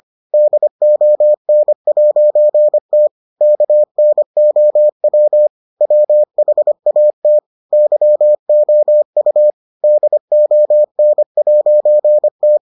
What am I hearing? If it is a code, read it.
Morse code, heard as a keyed tone, about 25 wpm, 600 Hz. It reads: DON'T KNOW WHAT YOU DON'T